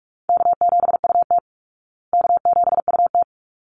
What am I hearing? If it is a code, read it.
Morse code, heard as a keyed tone, about 44 wpm, 700 Hz. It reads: K7UT K7UT
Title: Easy Call Signs Repeated 2x - 45wpm